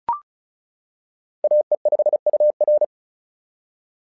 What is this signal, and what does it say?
Morse code, heard as a keyed tone, about 36 wpm, 600 Hz. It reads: AE5UR
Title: SOTA QSO Practice 35wpm